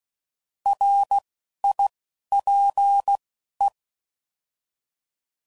Morse code, heard as a keyed tone, about 16 wpm, 800 Hz. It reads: R I P E
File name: Prefixes / By First Letter / Rs